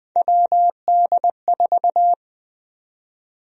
Morse code, heard as a keyed tone, about 21 wpm, 700 Hz. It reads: WD4
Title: USA Call Sign Prefixes - 20wpm